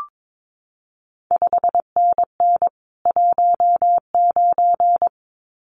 Morse code, heard as a keyed tone, about 22 wpm, 700 Hz. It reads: 5NN 19
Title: CQ WW DX Contest Exchange Encore - 22wpm